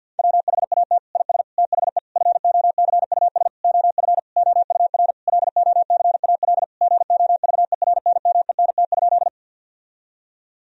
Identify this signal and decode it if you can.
Morse code, heard as a keyed tone, about 50 wpm, 700 Hz. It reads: WHAT IS THE FOCUS OF OUR LOCAL GOVERNMENT?
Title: Sentences from Top 1000 Words - Review Part 4 - 50wpm